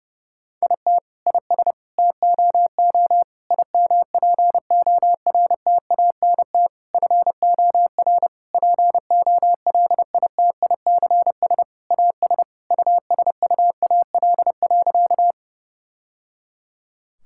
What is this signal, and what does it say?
Morse code, heard as a keyed tone, about 30 wpm, 700 Hz. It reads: IT IS TOO IMPORTANT FOR POLITICS AS USUAL.
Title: Sentences from Top 1000 Words - Review Part 6 - 30wpm